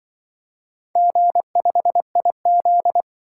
Morse code, heard as a keyed tone, about 24 wpm, 700 Hz. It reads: G5IZ
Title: Speed-Racing - Four Letter-Number Combinations - 20wpm